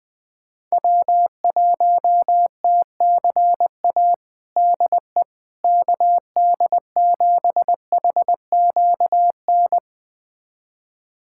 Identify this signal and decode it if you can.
Morse code, heard as a keyed tone, about 20 wpm, 700 Hz. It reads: W1TCA DE KD7HQN
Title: USA Callsign DE Callsign 20wpm